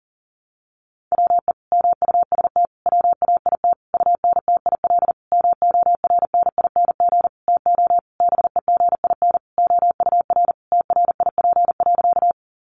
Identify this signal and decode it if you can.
Morse code, heard as a keyed tone, about 40 wpm, 700 Hz. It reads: WE MUST WAIT UNTIL MORNING TO BEGIN OUR TRIP.
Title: Sentences from Top 1300 Words 40wpm